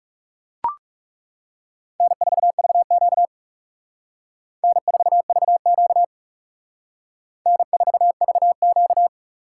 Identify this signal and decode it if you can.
Morse code, heard as a keyed tone, about 37 wpm, 700 Hz. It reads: N4VQ N4VQ N4VQ
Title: Speed Racing - USA 1x2 Call Signs 30wpm